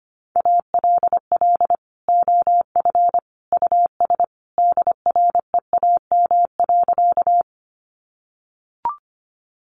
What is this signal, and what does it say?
Morse code, heard as a keyed tone, about 25 wpm, 700 Hz. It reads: ALL OF US DREAM.
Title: Sentences from Top 1500 Words 25wpm